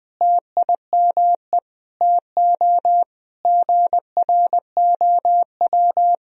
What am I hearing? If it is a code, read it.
Morse code, heard as a keyed tone, about 20 wpm, 700 Hz. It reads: TIME TO GROW
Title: Sentences from Top 1000 Words - Review Part 1 - 20wpm